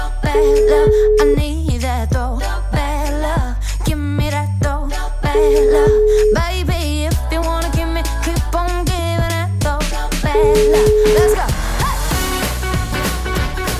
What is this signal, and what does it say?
Ringback tone (Europe, 425 Hz), with noise.